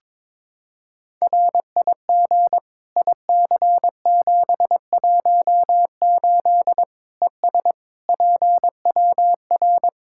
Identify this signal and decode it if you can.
Morse code, heard as a keyed tone, about 22 wpm, 700 Hz. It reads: RIG IC718 ES PWR
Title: Practice QSO - Exchange 2 - Info - 22wpm